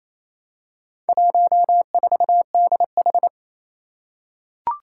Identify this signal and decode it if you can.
Morse code, heard as a keyed tone, about 27 wpm, 700 Hz. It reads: 14DH
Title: Four Challenging Letter-Number Combinations 28wpm